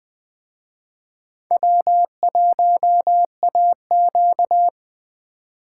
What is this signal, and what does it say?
Morse code, heard as a keyed tone, about 20 wpm, 700 Hz. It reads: W1AQ
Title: Speed Racing - USA 1x2 Call Signs 17wpm